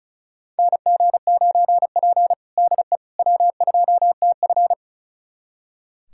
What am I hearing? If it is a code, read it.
Morse code, heard as a keyed tone, about 36 wpm, 690 Hz. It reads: NG9P DE W2TF
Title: USA Callsign DE Callsign - 4 Characters - 35wpm